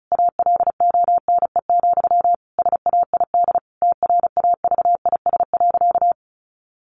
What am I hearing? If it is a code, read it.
Morse code, heard as a keyed tone, about 34 wpm, 700 Hz. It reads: ALONE, SAID TRAVIS.
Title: A Sound Of Thunder 35wpm